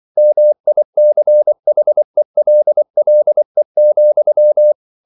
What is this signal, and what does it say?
Morse code, heard as a keyed tone, about 24 wpm, 600 Hz. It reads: MICHELLE,
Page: Ferie 24 wpm